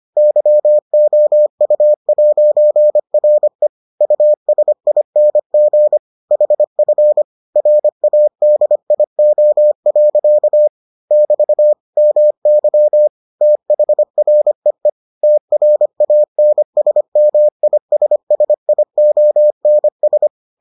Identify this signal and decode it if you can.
Morse code, heard as a keyed tone, about 25 wpm, 600 Hz. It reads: YOU'RE USING HF RADIO. <BT> MY THREE TRANSMISSIONS